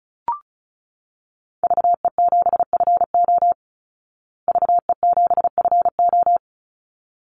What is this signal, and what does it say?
Morse code, heard as a keyed tone, about 34 wpm, 700 Hz. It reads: VE7FO VE7FO
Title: Call Signs Repeated 3x - 35wpm